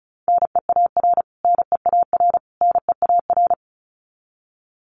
Morse code, heard as a keyed tone, about 34 wpm, 700 Hz. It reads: NEAR NEAR NEAR
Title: Top 500 Words Repeated 35wpm